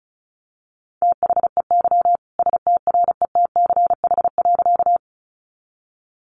Morse code, heard as a keyed tone, about 34 wpm, 700 Hz. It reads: THEY STRETCH.
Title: Easy Phrases 35wpm